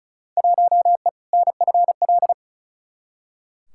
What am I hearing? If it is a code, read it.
Morse code, heard as a keyed tone, about 36 wpm, 700 Hz. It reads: 1E NFL
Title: Field Day Exchanges - 35wpm